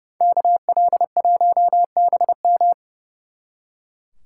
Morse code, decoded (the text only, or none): KL1BM